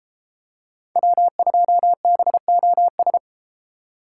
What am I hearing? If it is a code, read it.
Morse code, heard as a keyed tone, about 33 wpm, 700 Hz. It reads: W2BOS